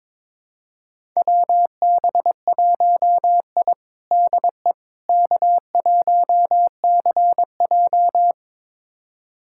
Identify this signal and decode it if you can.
Morse code, heard as a keyed tone, about 22 wpm, 700 Hz. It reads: WB1I DE K1CJ
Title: USA Callsign DE Callsign - 4 Characters - 22wpm